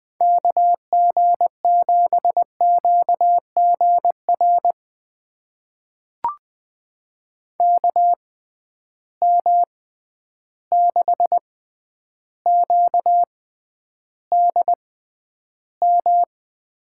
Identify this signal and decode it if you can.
Morse code, heard as a keyed tone, about 21 wpm, spaced as for 10 wpm, 700 Hz. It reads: KG7QGR KM6QDM
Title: USA Call Signs - 6 Characters - 6x Character Spacing - 20wpm